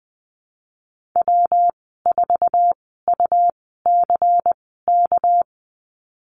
Morse code, heard as a keyed tone, about 20 wpm, spaced as for 15 wpm, 700 Hz. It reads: W4UCK
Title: 200 Lesson 32g - Letter k - Call Signs 20wpm